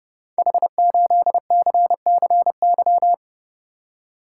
Morse code, heard as a keyed tone, about 29 wpm, 700 Hz. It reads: H8CCY
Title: Code Groups - 5 Letter:Numbers - 30wpm